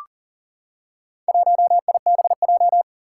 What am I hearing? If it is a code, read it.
Morse code, heard as a keyed tone, about 40 wpm, 700 Hz. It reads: 1IBJ
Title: Four Challenging Letter-Number Combinations 40wpm